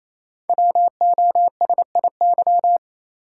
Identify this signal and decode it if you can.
Morse code, heard as a keyed tone, about 29 wpm, 700 Hz. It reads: WOSIY
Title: Code Groups - 5 Letter:Numbers - 28wpm